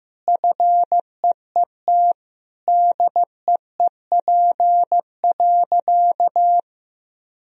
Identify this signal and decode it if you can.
Morse code, heard as a keyed tone, about 15 wpm, 700 Hz. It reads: FEET DEEP.
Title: Sentences from Top 1000 Words - Review Part 3 - 15wpm